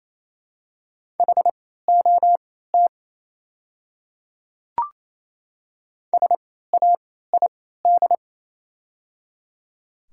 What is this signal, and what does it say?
Morse code, heard as a keyed tone, about 27 wpm, spaced as for 16 wpm, 690 Hz. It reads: HOT SAID
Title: 3x Character Spacing - Top 100 Words - 28wpm